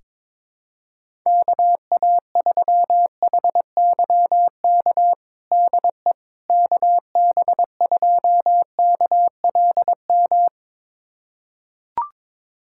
Morse code, heard as a keyed tone, about 22 wpm, 700 Hz. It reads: KA3HYK DE KB2KLM
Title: USA Callsign DE Callsign - 6 Characters - 22wpm